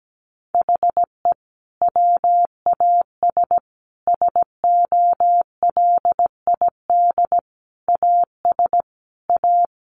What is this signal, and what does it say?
Morse code, heard as a keyed tone, about 17 wpm, 700 Hz. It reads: HE WAS SOLID AS A
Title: Sentences from Top 1500 Words 17wpm